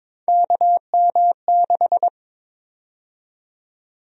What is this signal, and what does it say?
Morse code, heard as a keyed tone, about 22 wpm, 700 Hz. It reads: KM6
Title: USA Call Sign Prefixes - 22wpm